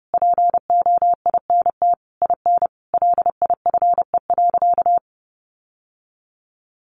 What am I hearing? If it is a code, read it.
Morse code, heard as a keyed tone, about 30 wpm, 700 Hz. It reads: POINT IN LIFE.
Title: Sentences from Top 1200 Words 30wpm